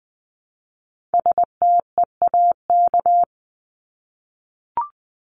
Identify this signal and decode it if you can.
Morse code, heard as a keyed tone, about 20 wpm, 700 Hz. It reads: STEAK
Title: Top 1251-1500 Words 20wpm